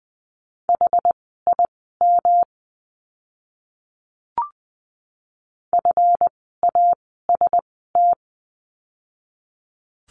Morse code, heard as a keyed tone, about 20 wpm, spaced as for 14 wpm, 700 Hz. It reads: HIM FAST
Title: 139 Lesson 23f - Letter f - Words 20wpm